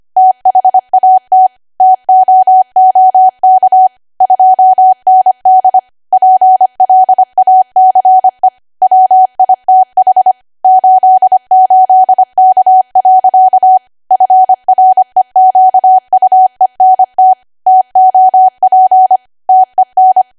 Morse code, heard as a keyed tone, about 25 wpm, 750 Hz. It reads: THAT TOOK 2ND PLACE WITH 88K. FREQUENT TOP TEN